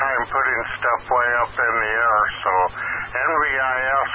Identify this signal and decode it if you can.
Emergency Alert System (SAME) tones, heard as harsh buzzing bursts, with speech. It is heard from 0.4 to 3.3 s.